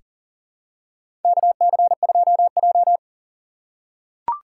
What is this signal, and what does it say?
Morse code, heard as a keyed tone, about 40 wpm, 700 Hz. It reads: KC2J
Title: USA Call Signs - 4 Characters - 3x Character Spacing - 40wpm